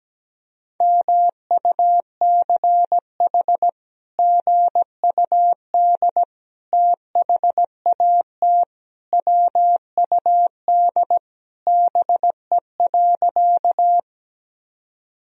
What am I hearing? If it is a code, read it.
Morse code, heard as a keyed tone, about 17 wpm, 700 Hz. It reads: MUCH GUD THAT WUD BE.
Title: Sentences from Top 200 Words and Abbreviations 17wpm